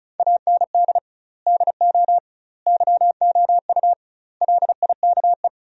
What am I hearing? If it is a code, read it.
Morse code, heard as a keyed tone, about 36 wpm, 700 Hz. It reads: AND DO YOU LIKE
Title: 2x Word-Spacing - Sentences from Top 100 Words - 35wpm